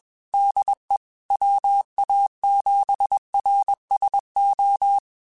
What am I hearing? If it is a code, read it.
Morse code, heard as a keyed tone, about 21 wpm, 800 Hz. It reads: DE WA7RSO
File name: QSOs / QSOs-mp3s / WD0FBT 1@20